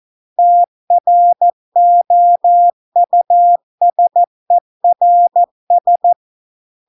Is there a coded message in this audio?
Yes: Morse code from 0.4 to 6.1 s, heard as a keyed tone, about 14 wpm, 700 Hz. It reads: TROUSERS